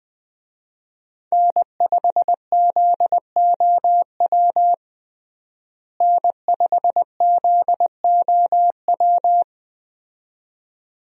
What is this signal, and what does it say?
Morse code, heard as a keyed tone, about 20 wpm, 700 Hz. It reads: N5ZOW N5ZOW
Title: USA Call Signs - 5 Characters - Repeated 2x - 20wpm